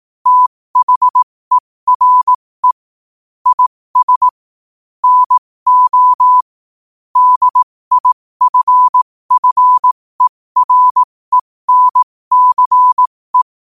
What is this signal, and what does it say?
Morse code, heard as a keyed tone, about 18 wpm, spaced as for 15 wpm, 1000 Hz. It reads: THERE IS NO DIFFERENCE
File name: qotdfiles / 15wpm